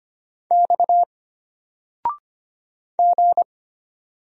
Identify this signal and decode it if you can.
Morse code, heard as a keyed tone, about 25 wpm, 700 Hz. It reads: X G
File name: Single-Letter-Number-25wpm